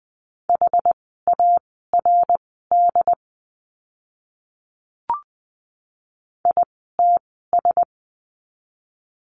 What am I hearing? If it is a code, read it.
Morse code, heard as a keyed tone, about 20 wpm, spaced as for 15 wpm, 700 Hz. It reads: HARD ITS
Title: 101 Lesson 17f - Letter u - Words 20wpm